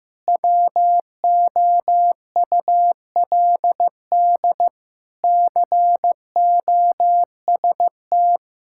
Morse code, heard as a keyed tone, about 16 wpm, 690 Hz. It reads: WOULD COST
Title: Sentences from Top 1000 Words - Review Part 5 - 15wpm